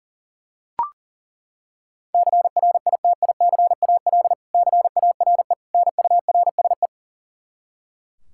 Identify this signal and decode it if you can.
Morse code, heard as a keyed tone, about 40 wpm, 700 Hz. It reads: CRITICAL CARE NURSE
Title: Practice QSO - Exchange 3 - More Info - 40wpm